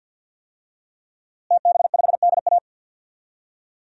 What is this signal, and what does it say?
Morse code, heard as a keyed tone, about 47 wpm, 700 Hz. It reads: TB5DA
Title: DXCC-100 Call Signs - 5 Characters - Repeated 2x - 50wpm